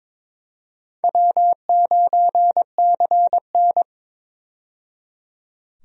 Morse code, heard as a keyed tone, about 22 wpm, 700 Hz. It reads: W9CN
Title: USA Call Signs - 4 Characters - 22wpm